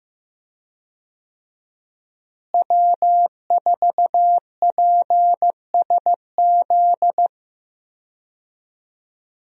Morse code, heard as a keyed tone, about 15 wpm, 700 Hz. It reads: W4PSZ
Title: USA Call Signs - 5 Characters - Repeated 3x - 15wpm